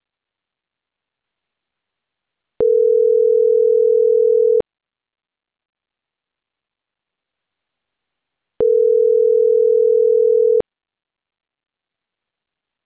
Ringback tone (North America), 440 + 480 Hz.